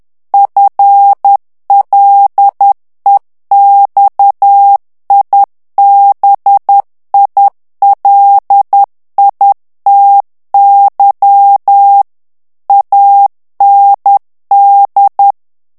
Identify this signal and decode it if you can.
Morse code, heard as a keyed tone, about 10 wpm, 800 Hz. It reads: FLEXIBILITY AND
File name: ANGLAIS texte clair 3 vitesse 10 mots minute